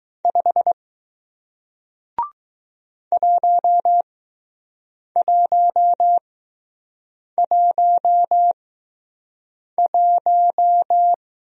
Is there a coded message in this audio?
Yes: Morse code from 0.2 to 11.1 s, heard as a keyed tone, about 20 wpm, spaced as for 6 wpm, 700 Hz. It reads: 5 1111